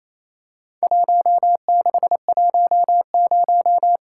Morse code, heard as a keyed tone, about 27 wpm, 700 Hz. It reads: 1610
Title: Worked All Europe DX Contest - QTC Exchange - 28wpm